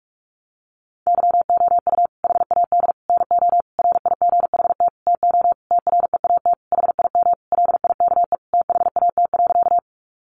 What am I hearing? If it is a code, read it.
Morse code, heard as a keyed tone, about 44 wpm, 700 Hz. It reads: YOU HAD NO RIGHT TO TREAT HIM LIKE THAT.
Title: Sentences from Top 1300 Words 45wpm